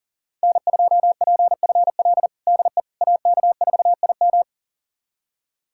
Morse code, heard as a keyed tone, about 40 wpm, 700 Hz. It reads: N2PFL DE AK4IM
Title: USA Callsign DE Callsign - 5 Characters - 40wpm